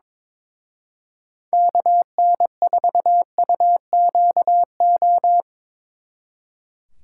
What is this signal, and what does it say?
Morse code, heard as a keyed tone, about 24 wpm, 700 Hz. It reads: KN4UQO
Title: Speed Racing - USA 2x3 Call Signs 22wpm